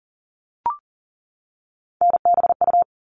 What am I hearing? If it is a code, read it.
Morse code, heard as a keyed tone, about 40 wpm, 700 Hz. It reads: NBU
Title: All Cut Numbers - 3-Digit 40wpm